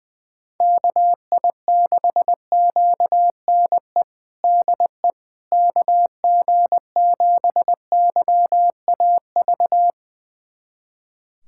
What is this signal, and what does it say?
Morse code, heard as a keyed tone, about 21 wpm, 700 Hz. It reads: KI6QNE DE KG7YAV
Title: USA Callsign DE Callsign 20wpm